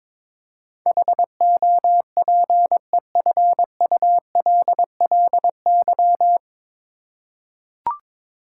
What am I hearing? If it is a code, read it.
Morse code, heard as a keyed tone, about 21 wpm, 700 Hz. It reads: HOPEFULLY
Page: Morse Code Ninja practice 22wpm